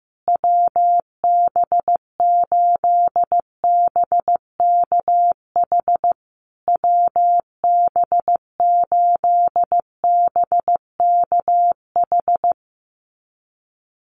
Morse code, heard as a keyed tone, about 15 wpm, 700 Hz. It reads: WB8BKH WB8BKH
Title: USA Call Signs Repeated 3x 15wpm